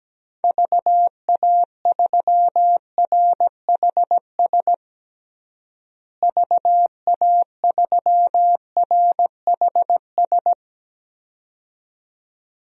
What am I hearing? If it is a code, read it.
Morse code, heard as a keyed tone, about 17 wpm, 700 Hz. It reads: VA3RHS VA3RHS
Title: DXCC-100 Call Signs - 6 Characters - Repeated 2x - 17wpm